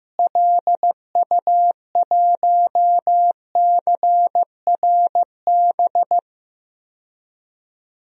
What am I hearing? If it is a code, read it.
Morse code, heard as a keyed tone, about 15 wpm, 700 Hz. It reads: LU1CRB
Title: DXCC-100 Call Signs - 6 Characters - Repeated 2x - 15wpm